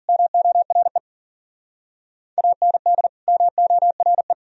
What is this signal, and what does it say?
Morse code, heard as a keyed tone, about 51 wpm, 700 Hz. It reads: MORE AND MORE